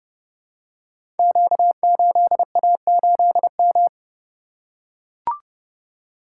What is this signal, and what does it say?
Morse code, heard as a keyed tone, about 30 wpm, 700 Hz. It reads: Q8A8M
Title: Code Groups - 5 Letter:Numbers - 30wpm